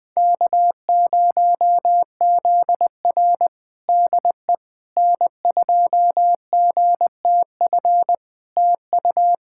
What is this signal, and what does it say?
Morse code, heard as a keyed tone, about 20 wpm, 700 Hz. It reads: K0ZR DE N2GTF TU
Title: Practice QSO - Exchange 4 - Ending - 20wpm